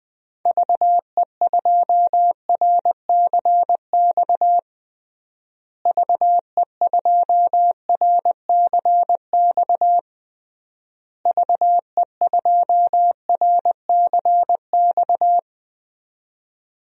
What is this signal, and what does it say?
Morse code, heard as a keyed tone, about 21 wpm, 700 Hz. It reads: VE2RCX VE2RCX VE2RCX
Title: DXCC-100 Call Signs - 6 Characters - Repeated 3x - 20wpm